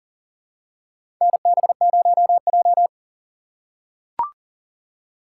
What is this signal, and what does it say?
Morse code, heard as a keyed tone, about 40 wpm, 700 Hz. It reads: NB0J
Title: USA Call Signs - 4 Characters - 3x Character Spacing - 40wpm